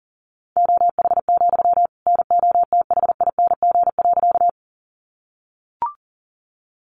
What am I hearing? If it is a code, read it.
Morse code, heard as a keyed tone, about 40 wpm, 700 Hz. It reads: OH, NOTHING.
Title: Sentences from Top 500 Words 40wpm